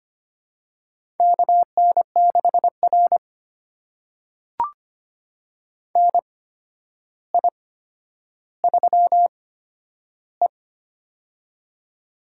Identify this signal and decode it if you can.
Morse code, heard as a keyed tone, about 25 wpm, spaced as for 11 wpm, 700 Hz. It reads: KN6R NI3E
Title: USA Call Signs - 4 Characters - 8x Character Spacing - 25wpm